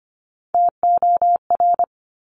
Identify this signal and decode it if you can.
Morse code, heard as a keyed tone, about 25 wpm, 700 Hz. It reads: TOR
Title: Top Three Letter Combinations in Words 25wpm